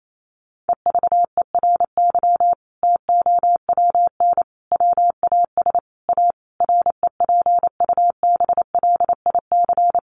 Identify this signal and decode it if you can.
Morse code, heard as a keyed tone, about 28 wpm, 700 Hz. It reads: EVERY TOWN WAS A REPUBLIC
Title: Sentences from Top 1500 Words 28wpm